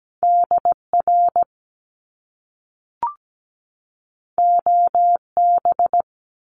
Morse code, heard as a keyed tone, about 17 wpm, 700 Hz. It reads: DR OB
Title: Top Two Letter Combinations in Words 17wpm